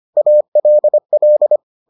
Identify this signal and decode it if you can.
Morse code, heard as a keyed tone, about 25 wpm, 600 Hz. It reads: ALL